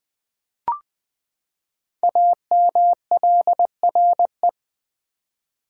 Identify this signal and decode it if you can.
Morse code, heard as a keyed tone, about 20 wpm, 700 Hz. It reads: AMLRE
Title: Code Groups - 5 Letters - 20wpm